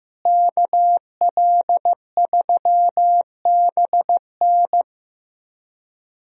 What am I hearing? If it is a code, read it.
Morse code, heard as a keyed tone, about 15 wpm, 700 Hz. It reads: KL3BN
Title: USA Call Signs - 5 Characters - Repeated 3x - 15wpm